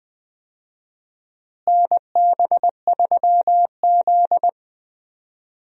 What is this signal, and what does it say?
Morse code, heard as a keyed tone, about 20 wpm, 700 Hz. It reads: NB3Z
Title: USA Call Signs - 4 Characters - Repeated 3x - 20wpm